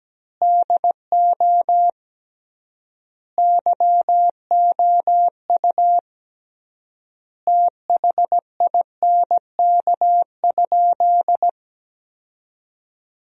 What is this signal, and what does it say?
Morse code, heard as a keyed tone, about 17 wpm, 700 Hz. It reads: DO YOU THINK?
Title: 3x Word-Spacing - Sentences from Top 200 Words - 17wpm